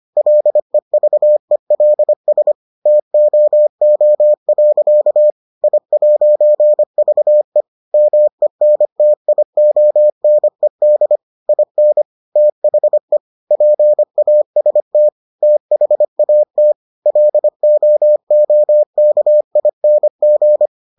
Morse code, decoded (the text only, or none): LEVELS TOO. I'VE MENTIONED IN THE PAST THAT LOOKING